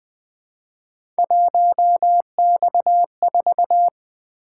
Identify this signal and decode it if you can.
Morse code, heard as a keyed tone, about 20 wpm, 700 Hz. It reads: 1X4
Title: Three Letter-Number Combinations - 20wpm